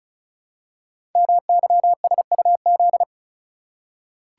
Morse code, heard as a keyed tone, about 36 wpm, 700 Hz. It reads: MYSUZ
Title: Code Groups - 5 Letters - 35wpm